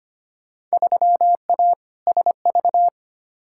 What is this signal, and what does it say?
Morse code, heard as a keyed tone, about 25 wpm, 700 Hz. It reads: 3A SV
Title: Field Day Exchanges - 25wpm